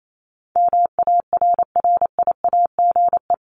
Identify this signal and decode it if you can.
Morse code, heard as a keyed tone, about 28 wpm, 700 Hz. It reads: MARRIAGE
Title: Top 751-1000 Words 28wpm